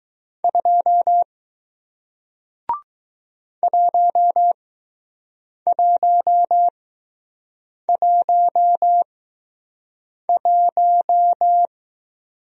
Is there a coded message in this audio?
Yes: Morse code from 0.4 to 11.7 s, heard as a keyed tone, about 20 wpm, spaced as for 6 wpm, 700 Hz. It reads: 2 1111